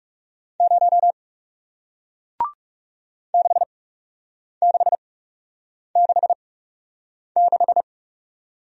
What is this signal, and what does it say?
Morse code, heard as a keyed tone, about 40 wpm, spaced as for 8 wpm, 700 Hz. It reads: 0 6666